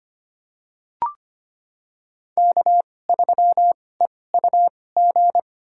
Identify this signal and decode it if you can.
Morse code, heard as a keyed tone, about 25 wpm, spaced as for 18 wpm, 700 Hz. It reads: K3EUG
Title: USA Call Signs - 2x Character Spacing - 25wpm